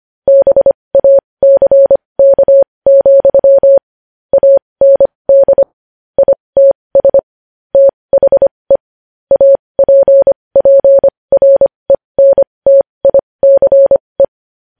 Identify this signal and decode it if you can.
Morse code, heard as a keyed tone, about 25 wpm, spaced as for 20 wpm, 550 Hz. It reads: BACK, AND ITS THE APPRENTICE
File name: podcast / News headlines-1x-25x20